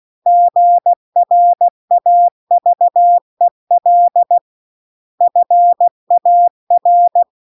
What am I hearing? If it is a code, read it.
Morse code, heard as a keyed tone, about 16 wpm, 700 Hz. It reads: GRAVEL FAR
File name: CW-The-Big-Twoheaded-River-2-16WPM